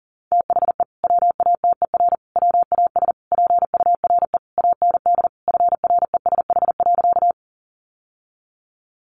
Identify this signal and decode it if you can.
Morse code, heard as a keyed tone, about 40 wpm, 700 Hz. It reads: THE WATER WAS PURE AND FRESH.
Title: Sentences from Top 1500 Words 40wpm